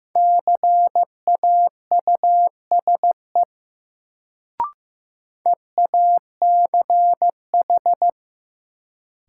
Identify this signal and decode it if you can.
Morse code, heard as a keyed tone, about 15 wpm, 700 Hz. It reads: CAUSE EACH
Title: Sentences from Top 200 Words - New Word Drill - 15wpm